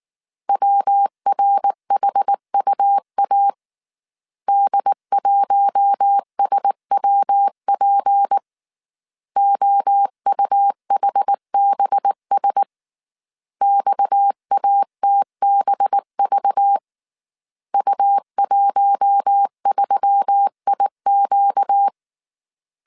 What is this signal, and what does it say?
Morse code, heard as a keyed tone, about 19 wpm, 800 Hz. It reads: WRHUA D1SWP OUHBS XATBV U13IQ